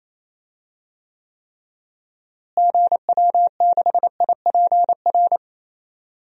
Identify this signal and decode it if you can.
Morse code, heard as a keyed tone, about 28 wpm, 700 Hz. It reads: GW6IPR
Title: DXCC-100 Call Signs - 6 Characters - Repeated 3x - 28wpm